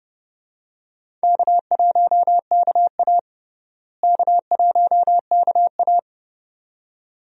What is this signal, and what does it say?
Morse code, heard as a keyed tone, about 30 wpm, 700 Hz. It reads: K1KA K1KA
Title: USA Call Signs - 4 Characters - Repeated 2x - 30wpm